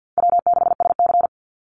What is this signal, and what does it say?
Morse code, heard as a keyed tone, about 50 wpm, 700 Hz. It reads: W6SC
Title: Easy Call Signs Repeated 2x - 50wpm